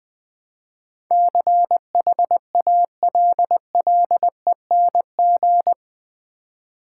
Morse code, heard as a keyed tone, about 20 wpm, 700 Hz. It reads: CHALLENG
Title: Build A Word - 4 Letters At A Time - 11-Letter Words - 20wpm